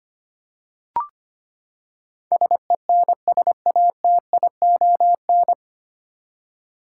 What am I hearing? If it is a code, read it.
Morse code, heard as a keyed tone, about 25 wpm, 700 Hz. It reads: SENSATION
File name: Nine Letter Words - 25wpm